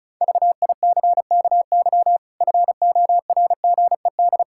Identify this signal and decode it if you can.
Morse code, heard as a keyed tone, about 34 wpm, 700 Hz. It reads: VICKY FORGED
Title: Pangram Challenge v2 - 35wpm